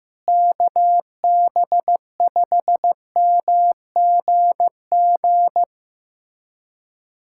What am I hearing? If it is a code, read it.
Morse code, heard as a keyed tone, about 16 wpm, 700 Hz. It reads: KB5MGG
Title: USA Callsign DE Callsign 15wpm